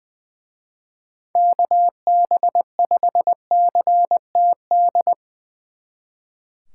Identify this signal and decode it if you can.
Morse code, heard as a keyed tone, about 22 wpm, 700 Hz. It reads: KB5CTD
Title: Speed Racing - USA 2x3 Call Signs 20wpm